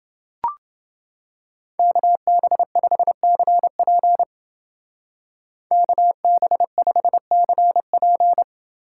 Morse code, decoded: KB5CP KB5CP

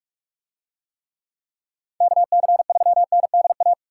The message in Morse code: KC3NDA

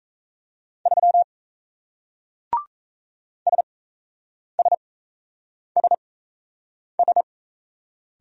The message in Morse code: 3 SSSS

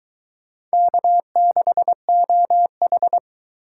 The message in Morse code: K6OH

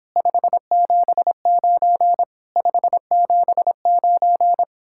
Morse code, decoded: 579 579

